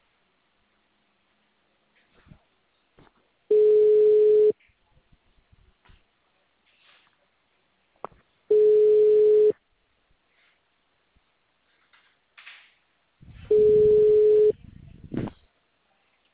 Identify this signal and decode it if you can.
Ringback tone (Europe, 425 Hz).